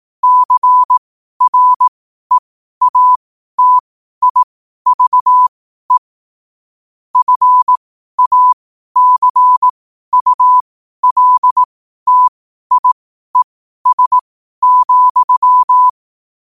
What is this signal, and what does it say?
Morse code, heard as a keyed tone, about 18 wpm, spaced as for 12 wpm, 1000 Hz. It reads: CREATIVE FACULTIES,
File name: qotdfiles / 12wpm